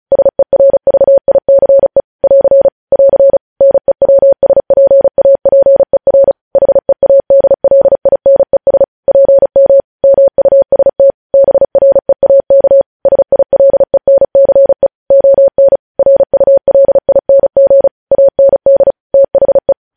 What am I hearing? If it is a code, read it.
Morse code, heard as a keyed tone, about 36 wpm, 550 Hz. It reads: SERVICE <AR> <AR> NEWSPAPER HEADLINES PM MUST BREAK SILENCE ON RULING AND THE
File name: podcast / News headlines-1x-35x35